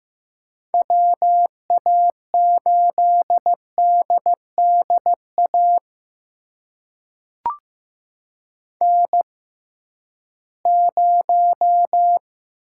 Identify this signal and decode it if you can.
Morse code, heard as a keyed tone, about 15 wpm, spaced as for 7 wpm, 700 Hz. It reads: WA8DD AN0
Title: USA Call Signs - 6x Character Spacing - 15wpm